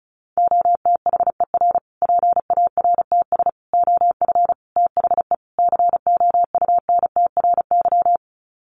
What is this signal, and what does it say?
Morse code, heard as a keyed tone, about 34 wpm, 700 Hz. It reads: OTHER PARTS OF THE COUNTRY
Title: Sets of 5 Words Encore 35wpm